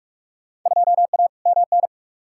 Morse code, heard as a keyed tone, about 46 wpm, 700 Hz. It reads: 2A MN
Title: Field Day Exchanges - 45wpm